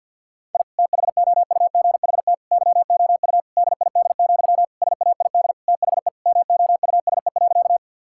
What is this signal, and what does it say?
Morse code, heard as a keyed tone, about 50 wpm, 700 Hz. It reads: I THOUGHT YOU DID, SAID THE MOUSE.
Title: Sentences from Top 1000 Words - Review Part 5 - 50wpm